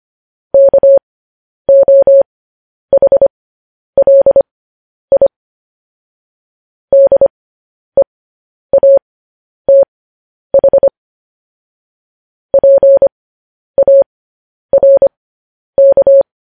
Morse code, heard as a keyed tone, about 25 wpm, spaced as for 10 wpm, 550 Hz. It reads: KOHLI DEATH PARK
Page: news-headlines-in-morse-code-at-10-wpm